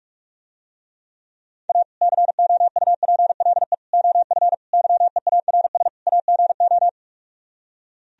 Morse code, heard as a keyed tone, about 45 wpm, 700 Hz. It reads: A COUPLE OF YEARS AGO